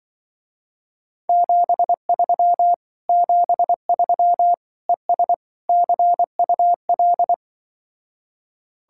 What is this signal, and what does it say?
Morse code, heard as a keyed tone, about 24 wpm, 700 Hz. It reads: 73 73 ES CUL
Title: Speed-Racing Common QSO Phrases - 20wpm